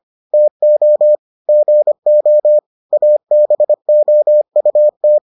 Morse code, heard as a keyed tone, about 25 wpm, 600 Hz. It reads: TO GO ABOUT